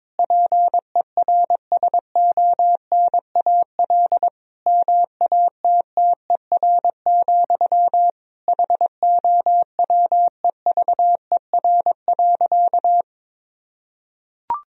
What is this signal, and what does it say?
Morse code, heard as a keyed tone, about 22 wpm, 700 Hz. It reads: PERSONAL MATTER, HOWEVER.
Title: Sentences from Top 1000 Words - Review Part 4 - 22wpm